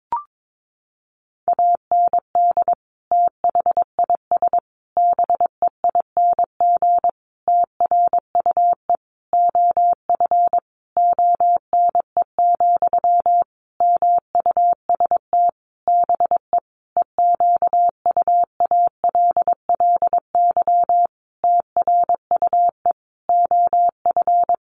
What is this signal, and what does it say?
Morse code, heard as a keyed tone, about 22 wpm, 700 Hz. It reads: AND THIS BEING TRUE OF ONE, MUST BE EQUALLY TRUE OF
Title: Sentences from Top 1400 Words 22wpm